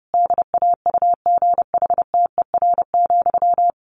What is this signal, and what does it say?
Morse code, heard as a keyed tone, about 30 wpm, 700 Hz. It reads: DAUGHTER,
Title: Sentences from Top 1500 Words 30wpm